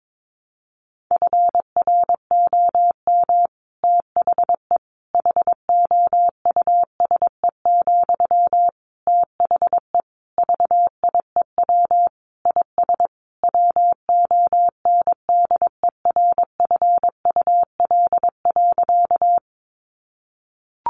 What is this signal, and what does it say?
Morse code, heard as a keyed tone, about 22 wpm, 700 Hz. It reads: FROM THE HOUSE, THE VIEW IS WONDERFUL.
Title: Sentences from Top 1100 Words 22wpm